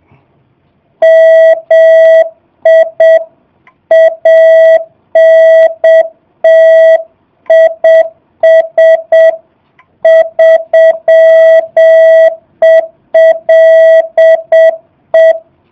Morse code, heard as a keyed tone, about 7 wpm, 660 Hz. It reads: MI ANT IS 3ELE
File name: jh8cbh / 2019cw / SP05- w